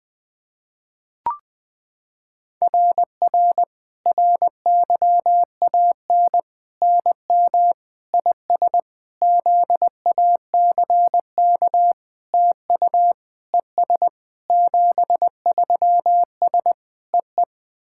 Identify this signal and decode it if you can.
Morse code, heard as a keyed tone, about 21 wpm, 700 Hz. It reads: RR RYAN NM IS ZACK TU ES 73S EE
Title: SOTA QSO Practice 20wpm